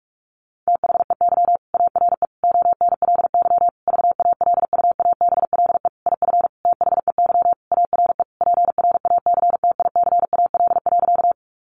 Morse code, heard as a keyed tone, about 44 wpm, 700 Hz. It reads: THEY ARE ONLY VALUABLE IF THEY ARE PRACTICAL.
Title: Sentences from Top 1400 Words 45wpm